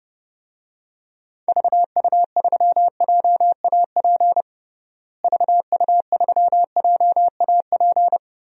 Morse code, heard as a keyed tone, about 30 wpm, 700 Hz. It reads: VU3JAP VU3JAP
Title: DXCC-100 Call Signs - 6 Characters - Repeated 3x - 30wpm